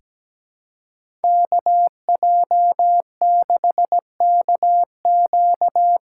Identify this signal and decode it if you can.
Morse code, heard as a keyed tone, about 17 wpm, 700 Hz. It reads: KJ6KQ
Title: USA Call Signs - 8x Character Spacing - 17wpm